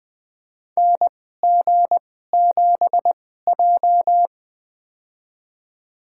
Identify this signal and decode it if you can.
Morse code, heard as a keyed tone, about 20 wpm, spaced as for 14 wpm, 700 Hz. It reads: NG7J
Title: USA Call Signs - 2x Character Spacing - 20wpm